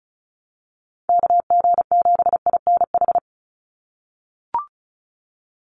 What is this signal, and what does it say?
Morse code, heard as a keyed tone, about 34 wpm, 700 Hz. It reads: KG7INH
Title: USA Call Signs 35wpm